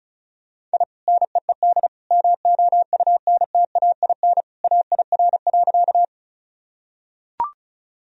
Morse code, heard as a keyed tone, about 34 wpm, 700 Hz. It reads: I NEED MOUNTAIN AIR.
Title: Sentences from Top 1000 Words - Review Part 1 - 35wpm